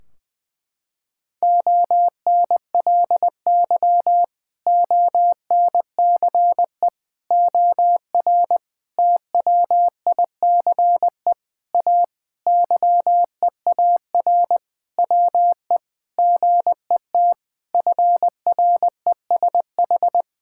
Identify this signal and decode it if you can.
Morse code, heard as a keyed tone, about 20 wpm, 700 Hz. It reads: ONLY ONCE OR TWICE A YEAR WE GET FRESH
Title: Sentences from Top 1600 Words 20wpm